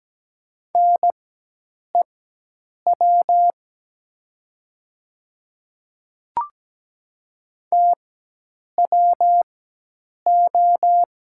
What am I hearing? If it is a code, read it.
Morse code, heard as a keyed tone, about 17 wpm, spaced as for 8 wpm, 700 Hz. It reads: NEW TWO
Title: 4x Character Spacing - Top 300 Words - 17wpm